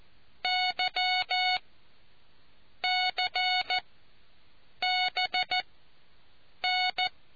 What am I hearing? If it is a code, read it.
Morse code, heard as a keyed tone, about 14 wpm, spaced as for 6 wpm, 740 Hz. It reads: YCBN